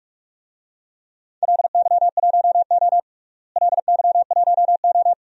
Morse code, heard as a keyed tone, about 41 wpm, 700 Hz. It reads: LY1O LY1O
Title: DXCC-100 Call Signs - 4 Characters - Repeated 3x - 45wpm